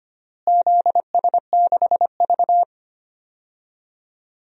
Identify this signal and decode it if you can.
Morse code, heard as a keyed tone, about 25 wpm, 700 Hz. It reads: ZS6V
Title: DXCC-100 Call Signs - 4 Characters - Repeated 2x - 25wpm